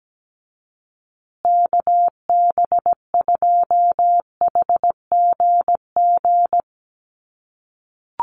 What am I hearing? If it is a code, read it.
Morse code, heard as a keyed tone, about 17 wpm, 700 Hz. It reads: KB2HGG
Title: USA Call Signs Repeated 2x 17wpm